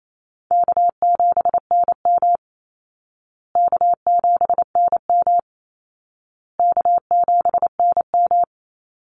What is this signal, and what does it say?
Morse code, heard as a keyed tone, about 28 wpm, 700 Hz. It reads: K7NM K7NM K7NM
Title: Easy Call Signs Repeated 3x - 28wpm